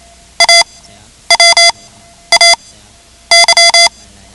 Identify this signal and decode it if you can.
Morse code, heard as a keyed tone, about 28 wpm, spaced as for 10 wpm, 2050 Hz. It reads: AWAY
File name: ham radio / CodeQuick / Code Quick 10